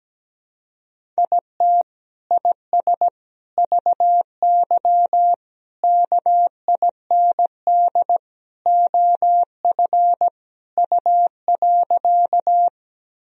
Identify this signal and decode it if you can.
Morse code, heard as a keyed tone, about 17 wpm, 700 Hz. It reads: IT IS VY KIND OF U.
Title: Sentences from Top 200 Words and Abbreviations 17wpm